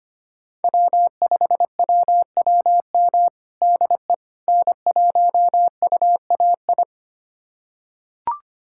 Morse code, heard as a keyed tone, about 25 wpm, 700 Hz. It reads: W5WWM DE N1UAI
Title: USA Callsign DE Callsign - 5 Characters - 25wpm